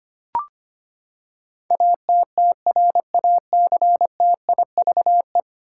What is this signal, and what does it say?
Morse code, heard as a keyed tone, about 25 wpm, 700 Hz. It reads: ATTRACTIVE
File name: Ten Letter Words - 25wpm